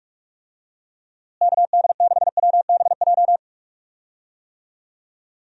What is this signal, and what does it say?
Morse code, heard as a keyed tone, about 47 wpm, 700 Hz. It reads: KD6WBJ
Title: USA Call Signs - 6 Characters - Repeated 3x - 45wpm